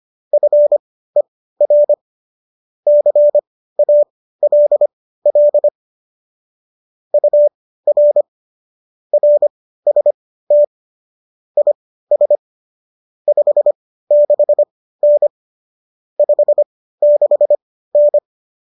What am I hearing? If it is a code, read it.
Morse code, heard as a keyed tone, about 25 wpm, spaced as for 15 wpm, 590 Hz. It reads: FER CALL UR RST IS 56N 56N
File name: qso213 15